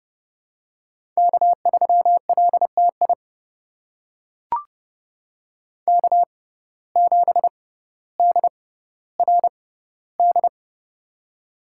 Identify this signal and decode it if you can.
Morse code, heard as a keyed tone, about 30 wpm, spaced as for 15 wpm, 700 Hz. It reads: K3LTI K7DRD
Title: USA Call Signs - 5 Characters - 6x Character Spacing - 30wpm